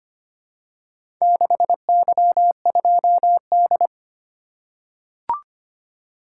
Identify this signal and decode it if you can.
Morse code, heard as a keyed tone, about 25 wpm, 700 Hz. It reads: 6Y2D
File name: practice / 25wpm / Four Letter-Number Combinations - 25wpm